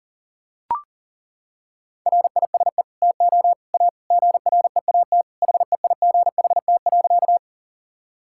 Morse code, heard as a keyed tone, about 40 wpm, 700 Hz. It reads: RISE TO A GREAT HEIGHT.
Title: Sentences from Top 1600 Words 40wpm